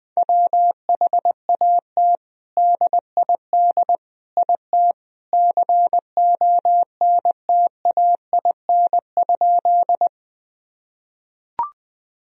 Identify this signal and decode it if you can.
Morse code, heard as a keyed tone, about 20 wpm, 700 Hz. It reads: WHAT DID IT CONTAIN?
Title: Sentences from Top 1000 Words - Review Part 2 - 20wpm